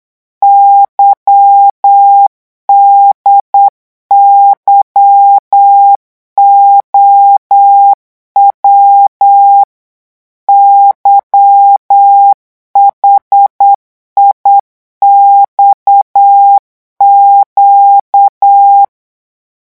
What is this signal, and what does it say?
Morse code, heard as a keyed tone, about 8 wpm, 800 Hz. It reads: YDYOW YHIXQ
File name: morse mp3 / revision8-vitesse 8 mots